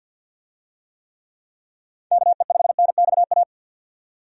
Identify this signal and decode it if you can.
Morse code, heard as a keyed tone, about 47 wpm, 700 Hz. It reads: KE5NXA
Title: USA Call Signs - 6 Characters - 50wpm